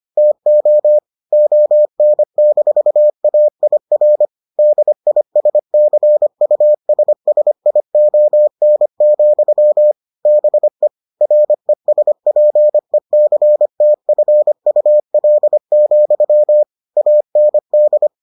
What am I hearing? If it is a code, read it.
Morse code, heard as a keyed tone, about 25 wpm, 600 Hz. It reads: TO ON-AIR DISCUSSION, BE RESPECTFUL, AND